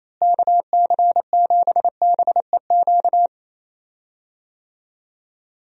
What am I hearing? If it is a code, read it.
Morse code, heard as a keyed tone, about 28 wpm, 700 Hz. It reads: KC7BEQ
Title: USA Call Signs - 6 Characters - Repeated 2x - 28wpm